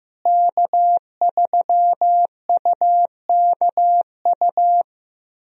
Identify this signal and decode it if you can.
Morse code, heard as a keyed tone, about 15 wpm, 700 Hz. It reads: K3UKU
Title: USA Call Signs - 5 Characters - Repeated 3x - 15wpm